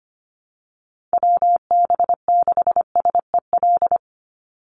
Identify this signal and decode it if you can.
Morse code, heard as a keyed tone, about 25 wpm, 700 Hz. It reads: WB6SEL
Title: Call Signs Repeated 3x - 25wpm